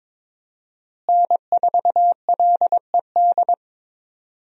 Morse code, heard as a keyed tone, about 22 wpm, 700 Hz. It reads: N4LED
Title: USA Call Signs - 5 Characters - Repeated 2x - 22wpm